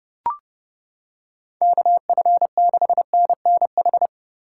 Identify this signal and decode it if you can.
Morse code, heard as a keyed tone, about 29 wpm, 700 Hz. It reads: KF6NNH